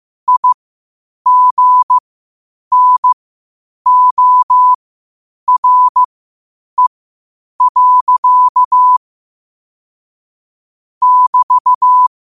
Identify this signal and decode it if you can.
Morse code, heard as a keyed tone, about 15 wpm, spaced as for 8 wpm, 1000 Hz. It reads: IGNORE. <BT>
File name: qotdfiles / 8wpm